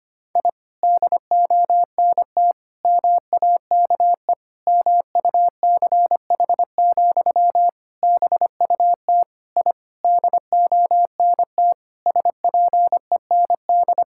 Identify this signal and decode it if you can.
Morse code, heard as a keyed tone, about 25 wpm, 700 Hz. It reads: I DONT MAKE MUCH, BUT I DONT SPEND
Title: Sentences from Top 1000 Words - Review Part 5 - 25wpm